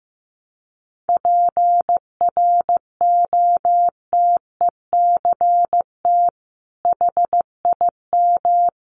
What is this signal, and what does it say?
Morse code, heard as a keyed tone, about 15 wpm, 700 Hz. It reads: PROTECT HIM
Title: Sentences from Top 900 Words 15wpm